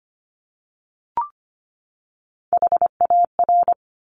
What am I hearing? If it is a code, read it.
Morse code, heard as a keyed tone, about 25 wpm, 700 Hz. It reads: HAR
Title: Top Three Letter Combinations in Words 25wpm